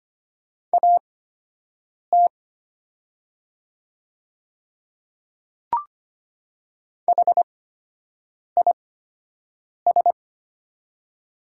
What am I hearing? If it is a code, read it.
Morse code, heard as a keyed tone, about 25 wpm, spaced as for 7 wpm, 700 Hz. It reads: AT HIS
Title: 8x Character Spacing - Top 100 Words - 25wpm